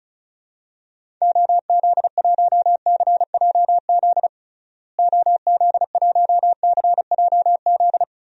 Morse code, heard as a keyed tone, about 36 wpm, 700 Hz. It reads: OZ1CJZ OZ1CJZ
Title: DXCC-100 Call Signs - 6 Characters - Repeated 3x - 35wpm